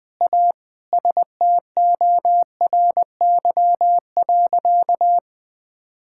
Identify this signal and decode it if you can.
Morse code, heard as a keyed tone, about 20 wpm, 700 Hz. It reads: A STORY.
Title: Sentences with 4-Word Phrases - 20wpm